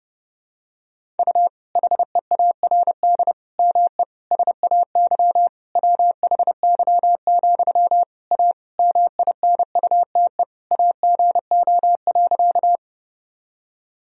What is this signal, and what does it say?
Morse code, heard as a keyed tone, about 32 wpm, 700 Hz. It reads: U HEARD ME SAY WHY, A MINUTE AGO.
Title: Sentences from Top 500 Words and Abbreviations 30wpm